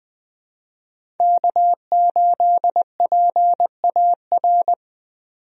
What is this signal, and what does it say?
Morse code, heard as a keyed tone, about 21 wpm, 700 Hz. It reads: K8PAR
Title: USA Call Signs - 5 Characters - 20wpm